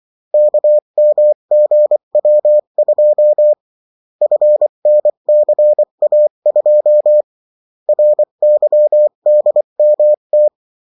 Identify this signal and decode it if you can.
Morse code, heard as a keyed tone, about 24 wpm, 600 Hz. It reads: KMGW2 FNCA2 RYDMT
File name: morse mp3 / revision9-vitesse 22 mots